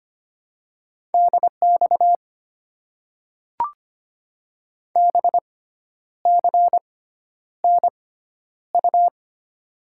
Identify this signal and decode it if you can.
Morse code, heard as a keyed tone, about 25 wpm, spaced as for 10 wpm, 700 Hz. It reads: DX BCNU
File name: Top 90 QSO Elements - 6x Character Spacing - 25wpm